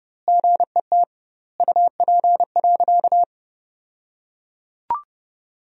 Morse code, heard as a keyed tone, about 30 wpm, 700 Hz. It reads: GET UP.
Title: 2x Word-Spacing - Sentences from Top 500 Words - 30wpm